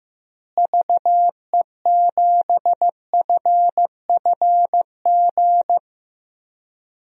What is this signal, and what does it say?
Morse code, heard as a keyed tone, about 15 wpm, 700 Hz. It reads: VE7FFG
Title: DXCC-100 Call Signs - 6 Characters - Repeated 2x - 15wpm